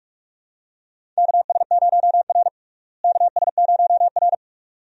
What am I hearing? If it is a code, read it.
Morse code, heard as a keyed tone, about 46 wpm, 700 Hz. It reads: KS0R KS0R
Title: USA Call Signs - 4 Characters - Repeated 2x - 45wpm